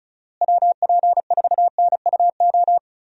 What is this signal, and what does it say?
Morse code, heard as a keyed tone, about 34 wpm, 700 Hz. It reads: WP4NUO
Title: USA Call Signs - 6 Characters - Repeated 3x - 35wpm